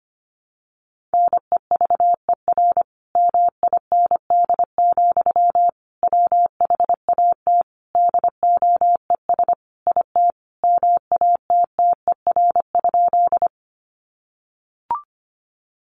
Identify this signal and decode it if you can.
Morse code, heard as a keyed tone, about 25 wpm, 700 Hz. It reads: NEVER MIND, WHAT DOES IT MATTER?
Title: Sentences from Top 700 Words 25wpm